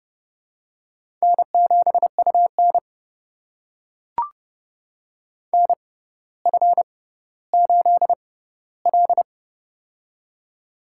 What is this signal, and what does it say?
Morse code, heard as a keyed tone, about 31 wpm, spaced as for 15 wpm, 700 Hz. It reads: N7UN NF8L
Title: USA Call Signs - 6x Character Spacing - 30wpm